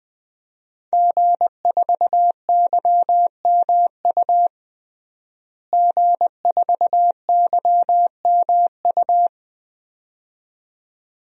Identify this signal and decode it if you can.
Morse code, heard as a keyed tone, about 20 wpm, 700 Hz. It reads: G4YMU G4YMU
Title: DXCC-100 Call Signs - 5 Characters - Repeated 2x - 20wpm